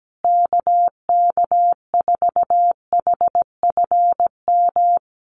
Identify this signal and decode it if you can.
Morse code, heard as a keyed tone, about 17 wpm, 700 Hz. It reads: KK4HFM
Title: USA Call Signs Repeated 3x 17wpm